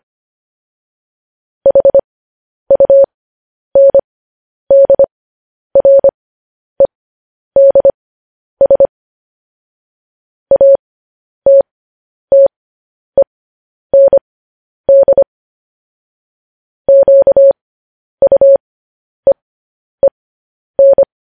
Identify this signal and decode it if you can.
Morse code, heard as a keyed tone, about 25 wpm, spaced as for 10 wpm, 550 Hz. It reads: HUNDREDS ATTEND QUEEN
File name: podcast / News headlines-1x-25x10